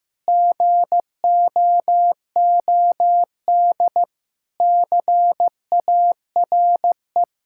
Morse code, heard as a keyed tone, about 15 wpm, 700 Hz. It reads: GOOD CARE
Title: Sentences from Top 1000 Words - Review Part 5 - 15wpm